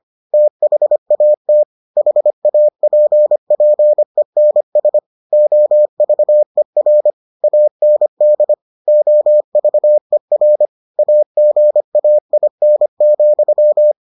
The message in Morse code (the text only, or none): THAT HAPPENS OVER AND OVER AGAIN,